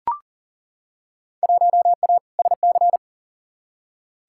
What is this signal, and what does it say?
Morse code, heard as a keyed tone, about 40 wpm, 700 Hz. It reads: 1A SC
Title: Field Day Exchanges - 40wpm